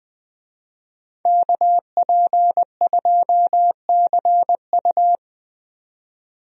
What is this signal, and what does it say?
Morse code, heard as a keyed tone, about 20 wpm, 700 Hz. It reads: KP2CU
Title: DXCC-100 Call Signs - 5 Characters - 20wpm